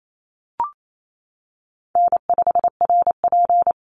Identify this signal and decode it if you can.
Morse code, heard as a keyed tone, about 28 wpm, 700 Hz. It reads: N5RP
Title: Easy Call Signs - 28wpm